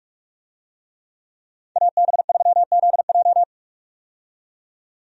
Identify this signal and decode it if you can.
Morse code, heard as a keyed tone, about 46 wpm, 700 Hz. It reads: AB3ZJ
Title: USA Call Signs - 5 Characters - 45wpm